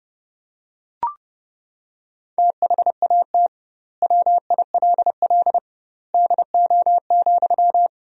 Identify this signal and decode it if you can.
Morse code, heard as a keyed tone, about 32 wpm, 700 Hz. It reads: THAT WILL DO,
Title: 2x Word-Spacing - Sentences from Top 100 Words - 30wpm